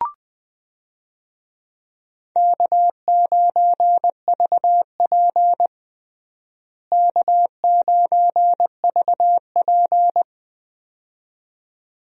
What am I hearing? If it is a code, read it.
Morse code, heard as a keyed tone, about 21 wpm, 700 Hz. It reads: K9VP K9VP
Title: USA Call Signs - 4 Characters - Repeated 2x - 20wpm